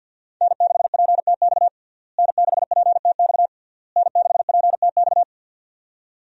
Morse code, heard as a keyed tone, about 46 wpm, 700 Hz. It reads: N6PTX N6PTX N6PTX
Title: USA Call Signs - 5 Characters - Repeated 3x - 50wpm